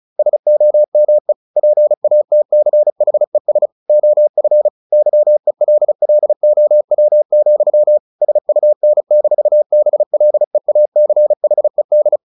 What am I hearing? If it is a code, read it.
Morse code, heard as a keyed tone, about 34 wpm, 600 Hz. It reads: SOME PATCHES OF YELLOW, SUN-BLEACHED
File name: the old man and the sea / the old man and the sea 1 35wpm